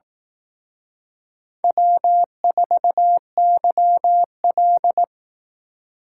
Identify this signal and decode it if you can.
Morse code, heard as a keyed tone, about 19 wpm, 700 Hz. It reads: W4YL